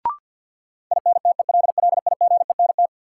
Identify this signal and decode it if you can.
Morse code, heard as a keyed tone, about 50 wpm, 700 Hz. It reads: INTELLIGENT
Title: Sentences from Top 1000 Words - Review Part 7 - New Word Drill - 50wpm